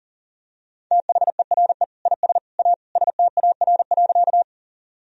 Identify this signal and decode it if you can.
Morse code, heard as a keyed tone, about 40 wpm, 700 Hz. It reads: THERE IS A STAR.
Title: Sentences from Top 1000 Words - Review Part 2 - 40wpm